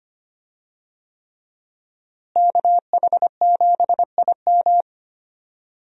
Morse code, heard as a keyed tone, about 25 wpm, 700 Hz. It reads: KH7IM
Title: USA Call Signs - 5 Characters - Repeated 3x - 25wpm